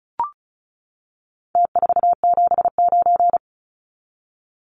Morse code, heard as a keyed tone, about 34 wpm, 700 Hz. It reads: T479
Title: Numbers - Contest Style - 4-Digit 35wpm